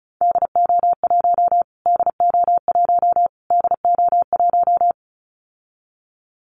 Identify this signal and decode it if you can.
Morse code, heard as a keyed tone, about 34 wpm, 700 Hz. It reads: DO1 DO1 DO1
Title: Callsign Prefixes for Top 25 DXCC - Repeated 3x - 35wpm